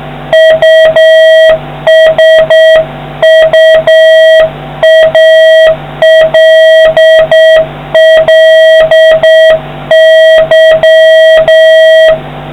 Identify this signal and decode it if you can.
Morse code, heard as a keyed tone, about 7 wpm, 1900 Hz, with noise. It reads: USUALLY